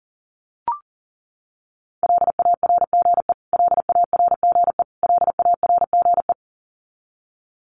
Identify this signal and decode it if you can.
Morse code, heard as a keyed tone, about 40 wpm, 700 Hz. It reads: LARGE LARGE LARGE
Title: Top 500 Words Repeated 40wpm